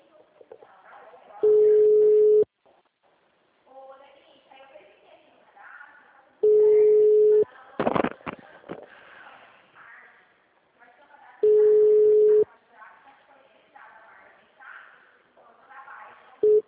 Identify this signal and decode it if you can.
Ringback tone (Europe, 425 Hz), with speech.